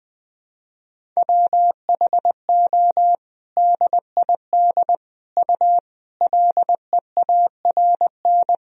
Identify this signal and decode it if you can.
Morse code, heard as a keyed tone, about 20 wpm, 700 Hz. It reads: WHO DID U LEARN
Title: Sentences from Top 300 Words and Abbreviations 20wpm